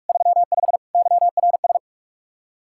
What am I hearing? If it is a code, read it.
Morse code, heard as a keyed tone, about 51 wpm, 700 Hz. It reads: 35 YRS
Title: Speed-Racing Common QSO Phrases Encore - 45wpm